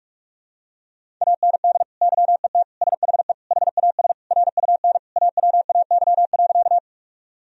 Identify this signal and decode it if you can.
Morse code, heard as a keyed tone, about 41 wpm, 700 Hz. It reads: AND YET SHE HAS RUN AWAY.
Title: Sentences from Top 1000 Words - Review Part 1 - 45wpm